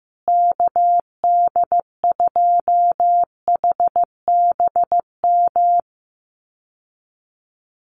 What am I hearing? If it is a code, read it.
Morse code, heard as a keyed tone, about 15 wpm, 700 Hz. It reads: KD2HBM
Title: USA Call Signs Repeated 3x 15wpm